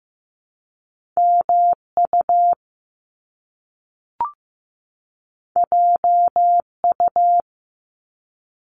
Morse code, heard as a keyed tone, about 15 wpm, 700 Hz. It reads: MU JU
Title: Top Two Letter Combinations in Words 15wpm